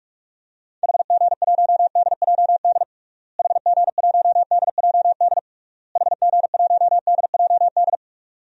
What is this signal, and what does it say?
Morse code, heard as a keyed tone, about 46 wpm, 700 Hz. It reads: HG1DJD HG1DJD HG1DJD
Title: DXCC-100 Call Signs - 6 Characters - Repeated 3x - 45wpm